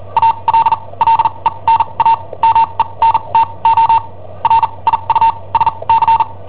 Morse code, heard as a keyed tone, about 43 wpm, spaced as for 28 wpm, 910 Hz. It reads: ALLENAMENTO RIUSC
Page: Cassetta esercizi preparata da I7OHP per IW5ELP